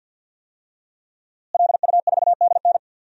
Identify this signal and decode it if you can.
Morse code, heard as a keyed tone, about 52 wpm, 700 Hz. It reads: LU4DN
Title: DXCC-100 Call Signs - 5 Characters - Repeated 3x - 50wpm